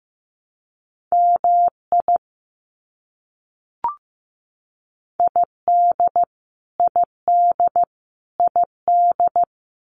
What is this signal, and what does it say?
Morse code, heard as a keyed tone, about 15 wpm, 700 Hz. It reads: MI ID ID ID
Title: US States Repeated 3x 15wpm